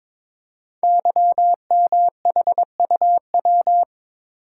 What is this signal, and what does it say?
Morse code, heard as a keyed tone, about 22 wpm, 700 Hz. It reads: YMHUW
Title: Code Groups - 5 Letter:Numbers - 22wpm